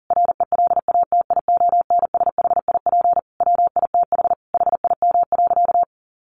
Morse code, heard as a keyed tone, about 40 wpm, 700 Hz. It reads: RELATIONSHIP WITH HIM.
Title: Sentences from Top 1100 Words 40wpm